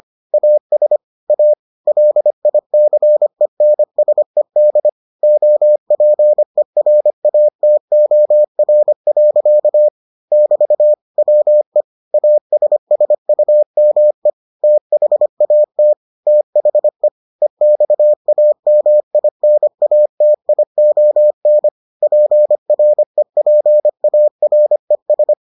Morse code, heard as a keyed tone, about 25 wpm, 600 Hz. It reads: AS A LICENSED OPERATOR. <BT> WE ASSUME THAT THE EXAMINATION PREPARES